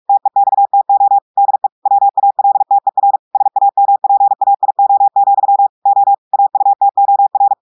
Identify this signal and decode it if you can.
Morse code, heard as a keyed tone, about 46 wpm, 800 Hz. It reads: TEXTO DE WALTER SAMPAIO, O AUTOR
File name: portugais texte en clair 1 vitesse 45 mots minutes